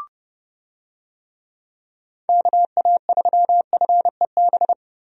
Morse code, heard as a keyed tone, about 29 wpm, 700 Hz. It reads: KA3FEB
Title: USA Call Signs - 6 Characters - Repeated 3x - 30wpm